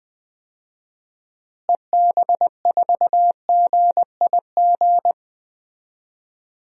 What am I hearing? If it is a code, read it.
Morse code, heard as a keyed tone, about 20 wpm, 700 Hz. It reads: EB4GIG
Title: DXCC-100 Call Signs - 6 Characters - Repeated 2x - 20wpm